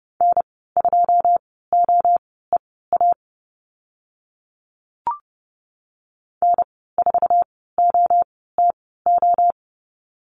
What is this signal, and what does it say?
Morse code, heard as a keyed tone, about 30 wpm, spaced as for 17 wpm, 700 Hz. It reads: N2OEA N4OTO
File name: Lesson 14g - Number 2 - Call Signs 30wpm